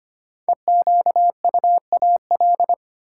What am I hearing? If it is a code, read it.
Morse code, heard as a keyed tone, about 25 wpm, 700 Hz. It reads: EQUAL
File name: 5 Letter Words - First 3 Letters Given - 25wpm